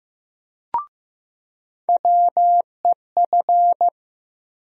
Morse code, heard as a keyed tone, about 15 wpm, 700 Hz. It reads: WEF
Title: USA Call Sign Postfixes - 3 Letters 15wpm